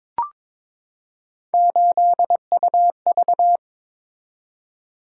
Morse code, heard as a keyed tone, about 22 wpm, 700 Hz. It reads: 8UV
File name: practice / 22wpm / Three Letter-Number Combinations - 22wpm